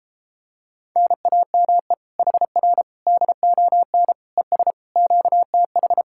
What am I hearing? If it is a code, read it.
Morse code, heard as a keyed tone, about 33 wpm, 700 Hz. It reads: NAME HR DON ES QTH